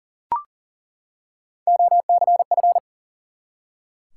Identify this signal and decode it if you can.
Morse code, heard as a keyed tone, about 42 wpm, 700 Hz. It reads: OCF
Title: Top 180 QSO Elements 40wpm